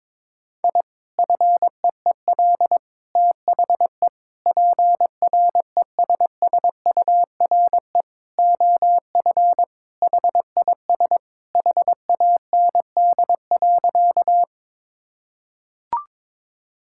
Morse code, heard as a keyed tone, about 22 wpm, 700 Hz. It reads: I FEEL THE PRESSURE OF HIS HAND.
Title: Sentences from Top 1000 Words - Review Part 6 - 22wpm